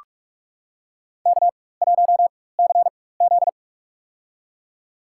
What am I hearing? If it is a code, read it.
Morse code, heard as a keyed tone, about 46 wpm, spaced as for 21 wpm, 700 Hz. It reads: K1CZ
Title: USA Call Signs - 4 Characters - 4x Character Spacing - 45wpm